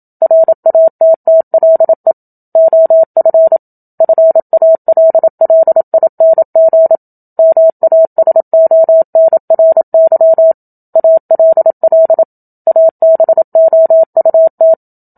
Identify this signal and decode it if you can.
Morse code, heard as a keyed tone, about 27 wpm, 650 Hz. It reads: RATTLE OF FALLING MASONRY ALL ABOUT
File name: Book Two / 01-Chapter 1 - 27 WPM